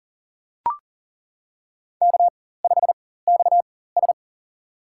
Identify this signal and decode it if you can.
Morse code, heard as a keyed tone, about 40 wpm, spaced as for 19 wpm, 700 Hz. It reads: K5XS
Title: USA Call Signs - 4x Character Spacing - 40wpm